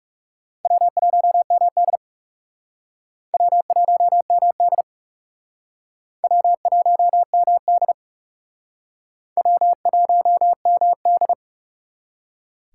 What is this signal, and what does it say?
Morse code, heard as a keyed tone, about 37 wpm, 700 Hz. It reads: W1MD W1MD W1MD W1MD
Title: Speed Racing - USA 1x2 Call Signs 30wpm